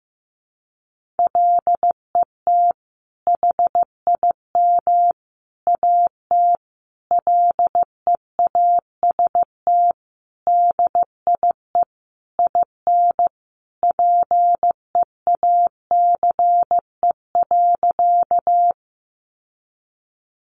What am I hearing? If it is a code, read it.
Morse code, heard as a keyed tone, about 15 wpm, 700 Hz. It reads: LET HIM AT LEAST DIE IN PEACE.
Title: Sentences from Top 1500 Words 15wpm